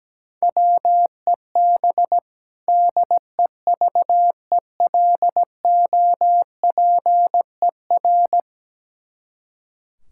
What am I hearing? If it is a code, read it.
Morse code, heard as a keyed tone, about 17 wpm, 700 Hz. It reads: WEB DEVELOPER
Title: K1USN SST QSO Practice 17wpm